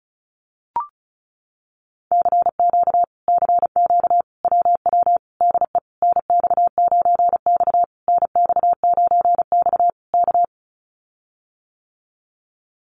Morse code, heard as a keyed tone, about 34 wpm, 700 Hz. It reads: CQ CQ WW DE NX9X NX9X K
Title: Common QSO Phrases Encore 35wpm